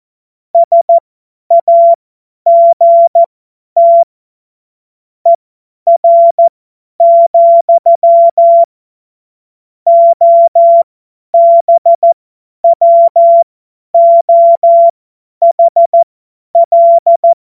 Morse code, heard as a keyed tone, about 14 wpm, spaced as for 10 wpm, 680 Hz. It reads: SAGT ER, OBWOHL
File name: Morsekurs / Witze / Witz 23. August 2022 50 F70